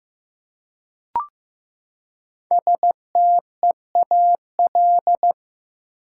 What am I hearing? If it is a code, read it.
Morse code, heard as a keyed tone, about 15 wpm, 700 Hz. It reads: STEAL
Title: Five Letter Words - 15wpm